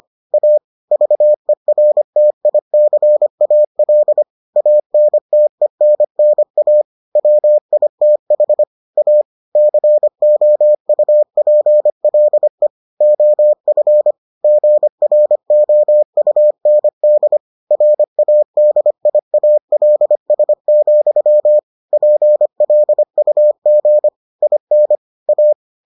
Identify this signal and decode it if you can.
Morse code, heard as a keyed tone, about 25 wpm, 600 Hz. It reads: A VERTICAL ANTENNA WITH A COUPLE OF GROUND RADIALS, PLUG IN A